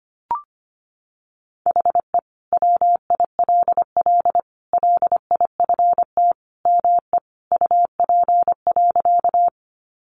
Morse code, heard as a keyed tone, about 25 wpm, 700 Hz. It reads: HE WILL LIFT ME UP.
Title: Sentences from Top 1000 Words 25wpm